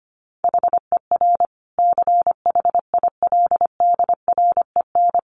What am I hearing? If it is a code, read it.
Morse code, heard as a keyed tone, about 25 wpm, 700 Hz. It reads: HER CHILDREN
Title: Sentences from Top 900 Words 25wpm